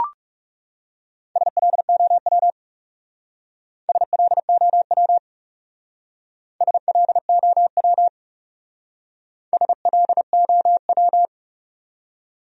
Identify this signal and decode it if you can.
Morse code, heard as a keyed tone, about 36 wpm, 700 Hz. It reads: SLOW SLOW SLOW SLOW
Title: Speed-Racing Top 500 Words 30wpm